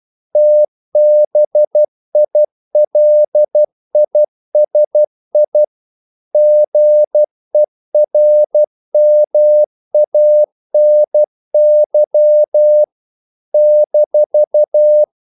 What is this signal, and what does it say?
Morse code, heard as a keyed tone, about 12 wpm, 600 Hz. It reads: TBILISI GERMANY -
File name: CW / Practice / NationalCapitals12